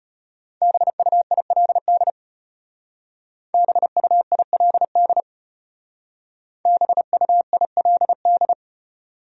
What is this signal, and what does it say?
Morse code, heard as a keyed tone, about 32 wpm, 700 Hz. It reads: BUILD BUILD BUILD